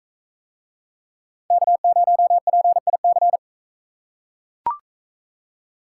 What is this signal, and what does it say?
Morse code, heard as a keyed tone, about 40 wpm, 700 Hz. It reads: K0PIC